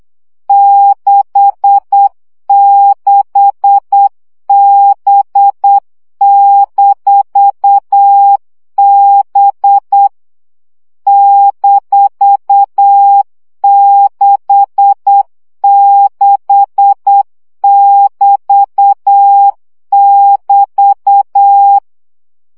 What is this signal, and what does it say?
Morse code, heard as a keyed tone, about 8 wpm, 800 Hz. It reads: 66B-B -66<BT><BT>
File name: CW gyakorlas b 6 egyenlo minusz 7 baud